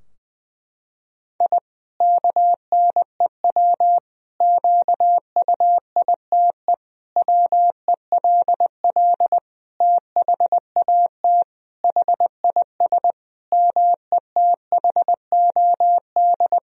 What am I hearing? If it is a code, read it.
Morse code, heard as a keyed tone, about 21 wpm, 700 Hz. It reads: I KNEW QUITE WELL THAT HIS METHOD
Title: Sentences from Top 1000 Words - Review Part 3 - 20wpm